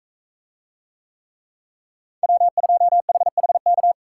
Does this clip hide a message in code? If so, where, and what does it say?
Yes: Morse code from 2.2 to 3.9 s, heard as a keyed tone, about 44 wpm, 700 Hz. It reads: W2HHK